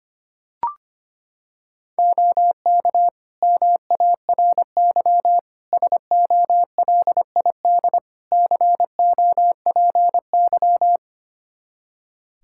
Morse code, heard as a keyed tone, about 25 wpm, 700 Hz. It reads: OK MARY SOLID COPY
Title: Practice QSO - Exchange 3 - More Info - 25wpm